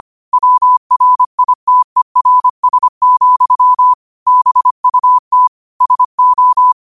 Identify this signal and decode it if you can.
Morse code, heard as a keyed tone, about 25 wpm, 1000 Hz. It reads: WRITERS, BUT SO
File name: qotdfiles / 25wpm